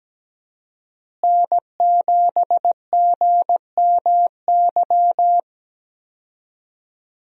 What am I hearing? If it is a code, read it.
Morse code, heard as a keyed tone, about 17 wpm, 700 Hz. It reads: N7GMY
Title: USA Call Signs - 5 Characters - Repeated 3x - 17wpm